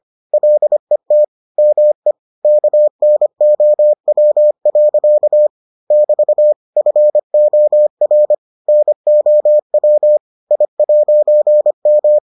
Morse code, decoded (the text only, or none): LET ME KNOW. <BT> FOR NOW I'M